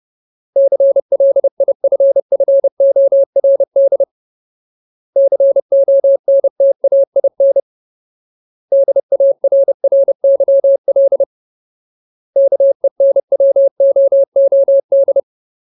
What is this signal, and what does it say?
Morse code, heard as a keyed tone, about 30 wpm, 550 Hz. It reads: CLIFFORD CONTAIN DARRYL KENWOOD
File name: Ultimate Word List / 30WPM Ultimate